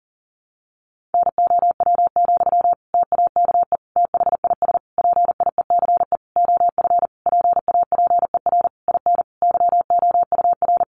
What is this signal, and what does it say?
Morse code, heard as a keyed tone, about 40 wpm, 700 Hz. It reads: NOW, TAKE THIS PIECE OF PAPER IN YOUR
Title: Sentences from Top 400 Words 40wpm